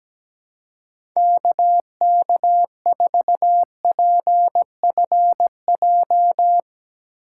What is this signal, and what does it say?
Morse code, heard as a keyed tone, about 17 wpm, 700 Hz. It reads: KK4PFJ
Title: Speed Racing - USA 2x3 Call Signs 17wpm